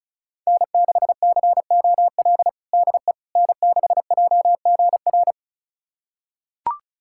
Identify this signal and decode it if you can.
Morse code, heard as a keyed tone, about 34 wpm, 700 Hz. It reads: N6COL DE N6JGR
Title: USA Callsign DE Callsign - 5 Characters - 35wpm